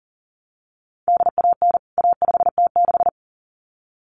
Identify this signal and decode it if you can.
Morse code, heard as a keyed tone, about 40 wpm, 700 Hz. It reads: DAN A5T6
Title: CWT Exchange 40wpm